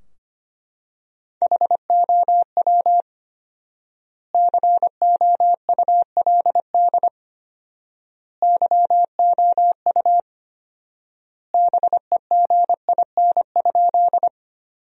Morse code, heard as a keyed tone, about 25 wpm, 700 Hz. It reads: HOW COULD YOU BEGIN?
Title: 4x Word-Spacing - Sentences from Top 300 Words - 25wpm